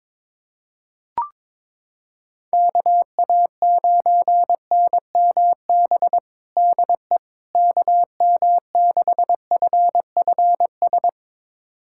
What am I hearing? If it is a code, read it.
Morse code, heard as a keyed tone, about 22 wpm, 700 Hz. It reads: KA9NMB DE KM6FFS
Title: USA Callsign DE Callsign 22wpm